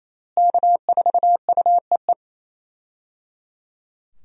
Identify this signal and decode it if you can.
Morse code, heard as a keyed tone, about 28 wpm, 700 Hz. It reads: K4UEE
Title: USA Call Signs - 5 Characters - Repeated 3x - 28wpm